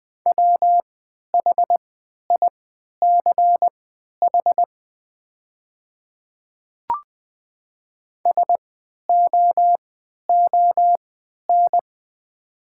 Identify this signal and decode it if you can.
Morse code, heard as a keyed tone, about 21 wpm, spaced as for 11 wpm, 690 Hz. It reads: WHICH SOON
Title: 3x Character Spacing - Top 500 Words - 20wpm